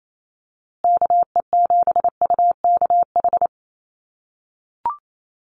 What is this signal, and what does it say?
Morse code, heard as a keyed tone, about 28 wpm, 700 Hz. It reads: KE7UKH
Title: USA Call Signs Repeated 3x 28wpm